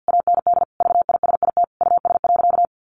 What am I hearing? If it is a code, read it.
Morse code, heard as a keyed tone, about 48 wpm, 700 Hz. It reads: AND VISIT US.
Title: Sentences from Top 1400 Words 50wpm